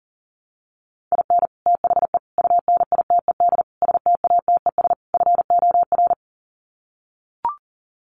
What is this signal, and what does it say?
Morse code, heard as a keyed tone, about 40 wpm, 700 Hz. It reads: IN THE UNITED STATES FOR
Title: Sets of 5 Words Encore2 40wpm